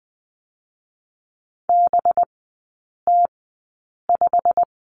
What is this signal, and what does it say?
Morse code, heard as a keyed tone, about 20 wpm, spaced as for 8 wpm, 700 Hz. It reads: BT5
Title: 184 Lesson 30e - Letter b - Three Characters 20wpm